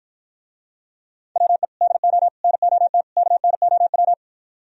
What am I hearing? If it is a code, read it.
Morse code, heard as a keyed tone, about 54 wpm, 700 Hz. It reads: WE DO NOT KNOW